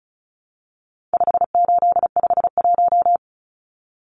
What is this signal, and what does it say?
Morse code, heard as a keyed tone, about 34 wpm, 700 Hz. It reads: 5851
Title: Numbers - 4-Digit 35wpm